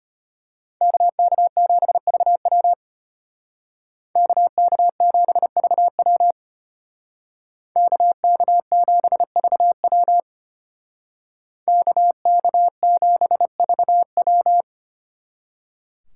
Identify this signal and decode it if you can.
Morse code, heard as a keyed tone, about 30 wpm, 700 Hz. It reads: KK7VW KK7VW KK7VW KK7VW
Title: Speed Racing - USA 2x2 Call Signs 25wpm